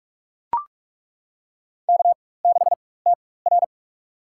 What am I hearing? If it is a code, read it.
Morse code, heard as a keyed tone, about 41 wpm, spaced as for 20 wpm, 700 Hz. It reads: K6TR
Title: USA Call Signs - 4 Characters - 4x Character Spacing - 45wpm